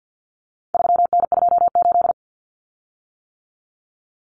Morse code, heard as a keyed tone, about 50 wpm, 700 Hz. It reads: 3N28
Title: Common Cut Numbers - 4-Digit 50wpm